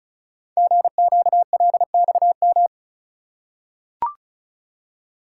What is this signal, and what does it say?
Morse code, heard as a keyed tone, about 34 wpm, 700 Hz. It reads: GQLXM
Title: Code Groups - 5 Letters - 35wpm